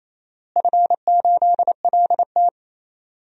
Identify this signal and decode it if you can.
Morse code, heard as a keyed tone, about 28 wpm, 700 Hz. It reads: F8LT
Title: DXCC-100 Call Signs - 4 Characters - Repeated 2x - 28wpm